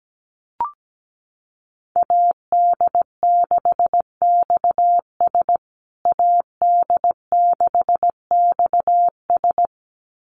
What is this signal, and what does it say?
Morse code, heard as a keyed tone, about 17 wpm, 700 Hz. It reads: AD6XS AD6XS
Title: USA Call Signs Repeated 2x 17wpm